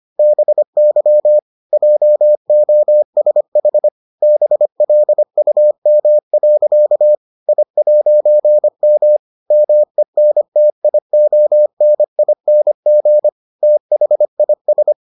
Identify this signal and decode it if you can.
Morse code, heard as a keyed tone, about 25 wpm, 600 Hz. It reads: BY JOSH BLUM. I'M MENTIONING THIS